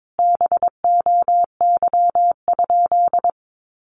Morse code, heard as a keyed tone, about 22 wpm, 700 Hz. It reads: BOY?
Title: Sentences from Top 1200 Words 22wpm